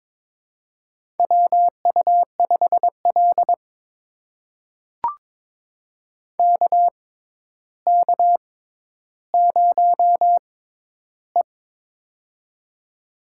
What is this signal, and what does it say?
Morse code, heard as a keyed tone, about 22 wpm, spaced as for 11 wpm, 700 Hz. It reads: WU5L KK0E
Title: USA Call Signs - 4 Characters - 6x Character Spacing - 22wpm